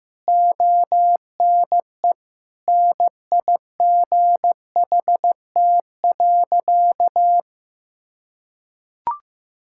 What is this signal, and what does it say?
Morse code, heard as a keyed tone, about 16 wpm, 700 Hz. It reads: ONE NIGHT.
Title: Sentences from Top 1600 Words 15wpm